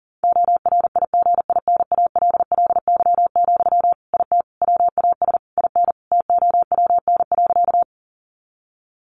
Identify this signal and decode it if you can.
Morse code, heard as a keyed tone, about 40 wpm, 700 Hz. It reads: ORIGINALLY, IT WAS IN TOWN.
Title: Sentences from Top 1500 Words 40wpm